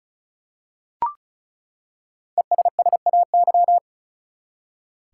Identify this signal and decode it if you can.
Morse code, heard as a keyed tone, about 36 wpm, 700 Hz. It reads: ESSAY
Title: Five Letter Words - 35wpm